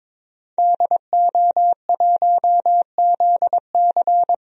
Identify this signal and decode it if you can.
Morse code, heard as a keyed tone, about 22 wpm, 700 Hz. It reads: DO1ZC
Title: DXCC-100 Call Signs - 5 Characters - Repeated 3x - 22wpm